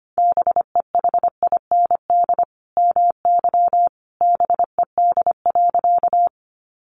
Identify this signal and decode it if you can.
Morse code, heard as a keyed tone, about 25 wpm, 700 Hz. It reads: BEHIND MY BED.
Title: Sentences from Top 500 Words 25wpm